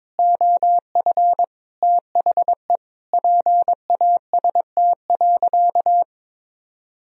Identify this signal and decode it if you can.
Morse code, heard as a keyed tone, about 22 wpm, 700 Hz. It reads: OF THE PAST.
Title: Sentences from Top 1000 Words - Review Part 3 - 22wpm